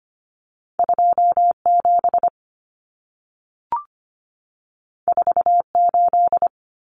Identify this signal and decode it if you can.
Morse code, heard as a keyed tone, about 25 wpm, 700 Hz. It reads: 27 48
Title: Common Cut Numbers - 2-Digit 25wpm